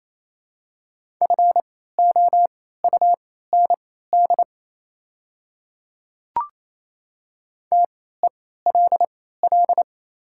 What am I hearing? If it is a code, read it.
Morse code, heard as a keyed tone, about 27 wpm, spaced as for 16 wpm, 700 Hz. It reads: FOUND TELL
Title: 3x Character Spacing - Top 300 Words - 28wpm